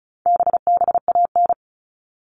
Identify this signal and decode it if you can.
Morse code, heard as a keyed tone, about 34 wpm, 700 Hz. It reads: BBAN
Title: All Cut Numbers - 4-Digit 35wpm